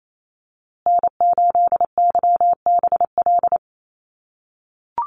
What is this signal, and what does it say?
Morse code, heard as a keyed tone, about 28 wpm, 700 Hz. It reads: N8YBL
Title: USA Call Signs Repeated 3x 28wpm